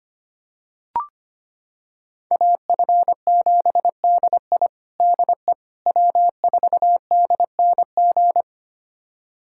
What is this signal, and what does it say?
Morse code, heard as a keyed tone, about 25 wpm, 700 Hz. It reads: AF7DI DE W4DNG
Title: USA Callsign DE Callsign - 5 Characters - 25wpm